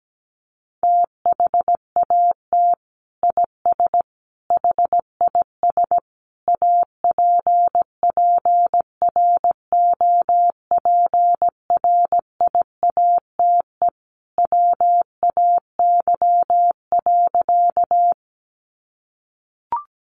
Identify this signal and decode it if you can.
Morse code, heard as a keyed tone, about 17 wpm, 700 Hz. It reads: THAT IS HIS APPROPRIATE WAY.
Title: Sentences from Top 900 Words 17wpm